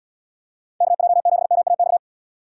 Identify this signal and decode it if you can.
Morse code, heard as a keyed tone, about 78 wpm, 700 Hz. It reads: KJ7MAX